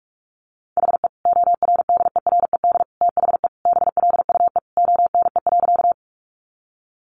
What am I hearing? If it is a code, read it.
Morse code, heard as a keyed tone, about 44 wpm, 700 Hz. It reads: HE ORDERED THE BLUE ONE.
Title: Additional Easy Phrases 45wpm